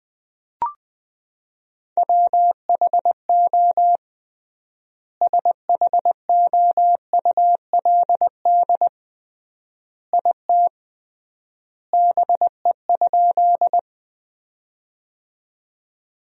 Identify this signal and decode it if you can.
Morse code, heard as a keyed tone, about 20 wpm, 700 Hz. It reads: WHO SHOULD IT BE?
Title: 3x Word-Spacing - Sentences from Top 300 Words - 20wpm